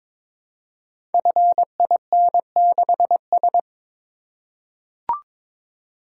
Morse code, heard as a keyed tone, about 22 wpm, 700 Hz. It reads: FIN6S
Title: Code Groups - 5 Letter:Numbers - 22wpm